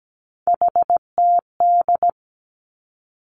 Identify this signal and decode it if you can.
Morse code, heard as a keyed tone, about 17 wpm, 700 Hz. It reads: HTD
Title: Three Challenging Letter-Number Combinations 17wpm